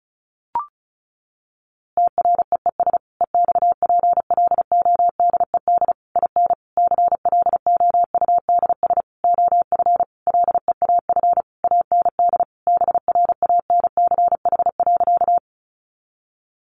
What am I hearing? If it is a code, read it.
Morse code, heard as a keyed tone, about 34 wpm, 700 Hz. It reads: TREES EXPLODED IN CLOUDS OF LEAF AND BRANCH.
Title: A Sound Of Thunder 35wpm